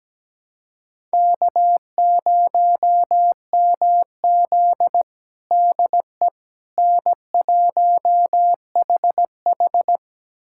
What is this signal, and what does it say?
Morse code, heard as a keyed tone, about 17 wpm, 700 Hz. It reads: K0MZ DE N1HH
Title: USA Callsign DE Callsign - 4 Characters - 17wpm